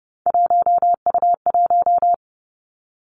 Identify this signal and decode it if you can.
Morse code, heard as a keyed tone, about 30 wpm, 700 Hz. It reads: 1U1
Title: Three Challenging Letter-Number Combinations 30wpm